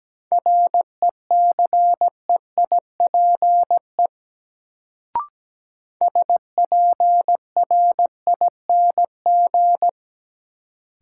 Morse code, heard as a keyed tone, about 17 wpm, 700 Hz. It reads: RECEIPE SPRING
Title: Sentences from Top 1200 Words - New Word Drill Alt - 17wpm